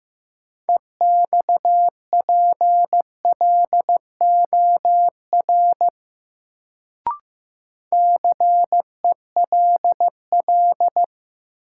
Morse code, heard as a keyed tone, about 15 wpm, 700 Hz. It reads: EXPLOR CELL
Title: Sentences from Top 1100 Words - New Word Drill - 15wpm